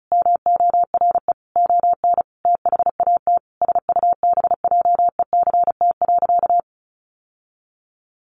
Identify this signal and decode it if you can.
Morse code, heard as a keyed tone, about 34 wpm, 700 Hz. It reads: MORE ON THAT SUBJECT.
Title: Sentences from Top 800 Words 35wpm